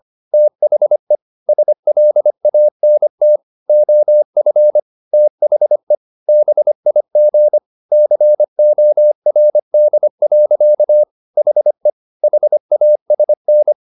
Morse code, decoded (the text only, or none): THE SLANT OF THE BIG CORD. HE HASN